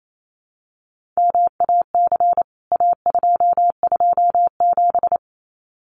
Morse code, heard as a keyed tone, about 28 wpm, 700 Hz. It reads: MAC A227
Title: CWT Exchange 28wpm